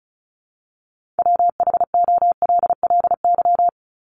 Morse code, heard as a keyed tone, about 34 wpm, 700 Hz. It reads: WHOLLY
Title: LY Suffix - Top 100 Words with Base Words 35wpm